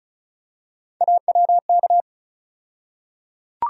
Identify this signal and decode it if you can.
Morse code, heard as a keyed tone, about 34 wpm, 700 Hz. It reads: AWK
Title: USA Call Sign Postfixes - 3 Letters 35wpm